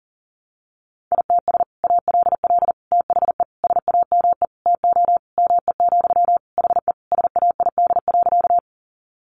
Morse code, heard as a keyed tone, about 40 wpm, 700 Hz. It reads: ITS ALL THE SAME TO ME, HE SAID.
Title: Sentences from Top 600 Words 40wpm